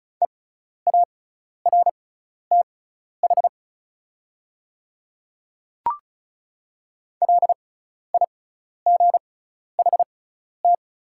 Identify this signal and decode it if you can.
Morse code, heard as a keyed tone, about 34 wpm, spaced as for 12 wpm, 700 Hz. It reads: EARTH LIGHT
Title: 6x Character Spacing - Top 200 Words - 35wpm